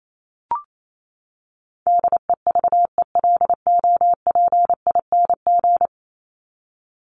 Morse code, heard as a keyed tone, about 28 wpm, 700 Hz. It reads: DEVELOPING
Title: ING Suffix - Top 100 Words with Base Words 28wpm